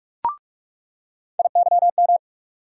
Morse code, heard as a keyed tone, about 46 wpm, 700 Hz. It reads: IYM
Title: USA Call Sign Postfixes - 3 Letters 45wpm